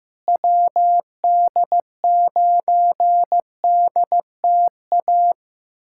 Morse code, heard as a keyed tone, about 15 wpm, 700 Hz. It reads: WD9DTA
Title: USA Callsign DE Callsign - 6 Characters - 15wpm